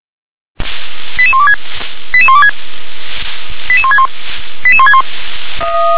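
Five-tone selective call (ZVEI), heard as quick runs of five tones, tones of about 70 ms. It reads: 88126 88126 88161 88161
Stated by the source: Five-tone ZVEI selective call 88666 received by SDR.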